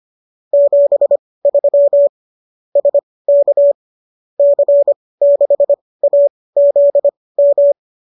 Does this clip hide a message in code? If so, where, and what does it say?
Yes: Morse code from 0.5 to 7.7 s, heard as a keyed tone, about 25 wpm, spaced as for 18 wpm, 570 Hz. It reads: 73 SK C6AZM